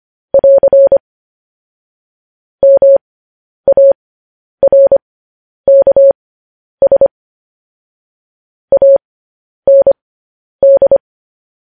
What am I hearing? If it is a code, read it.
Morse code, heard as a keyed tone, about 25 wpm, spaced as for 10 wpm, 550 Hz. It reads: <AR> MARKS AND
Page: News Headlines in Morse Code at 10 WPM